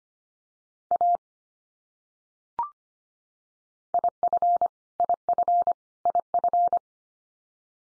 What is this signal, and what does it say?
Morse code, heard as a keyed tone, about 25 wpm, 700 Hz. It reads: A IF IF IF
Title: Top 100 Words 25wpm x3